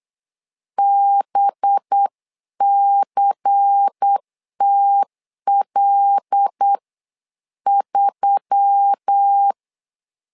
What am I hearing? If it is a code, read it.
Morse code, heard as a keyed tone, about 8 wpm, spaced as for 7 wpm, 800 Hz. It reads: BCTL3
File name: LEZIONI / 5 LBVC / 5 LBVC 8wpm 2935